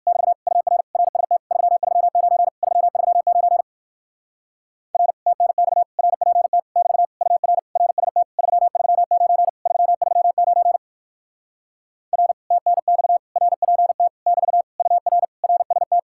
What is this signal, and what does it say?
Morse code, heard as a keyed tone, about 51 wpm, 700 Hz. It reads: <BT> UR RST 339 339 R TNX RPT <BT> UR RST 339 339 R TNX RPT <BT> UR RST
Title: Speed-Racing Common QSO Phrases Encore - 45wpm